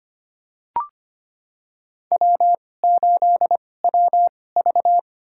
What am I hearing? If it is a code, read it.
Morse code, heard as a keyed tone, about 25 wpm, spaced as for 18 wpm, 700 Hz. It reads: W8WV
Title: USA Call Signs - 4 Characters - 2x Character Spacing - 25wpm